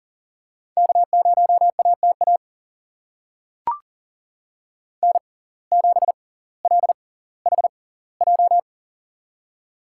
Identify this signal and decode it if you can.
Morse code, heard as a keyed tone, about 40 wpm, spaced as for 20 wpm, 700 Hz. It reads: K0ATA N7LHJ
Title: USA Call Signs - 5 Characters - 6x Character Spacing - 40wpm